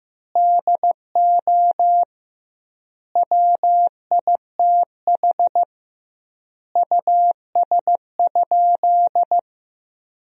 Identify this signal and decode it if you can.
Morse code, heard as a keyed tone, about 15 wpm, 700 Hz. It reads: DO WITH US?
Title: 2x Word-Spacing - Sentences from Top 200 Words - 15wpm